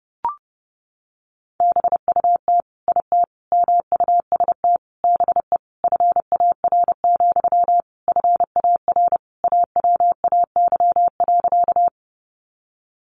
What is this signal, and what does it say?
Morse code, heard as a keyed tone, about 30 wpm, 700 Hz. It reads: BUT IT MUST BE FAR, FAR AWAY.
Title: Sentences from Top 600 Words 30wpm